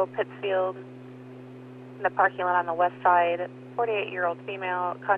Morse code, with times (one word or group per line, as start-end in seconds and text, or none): none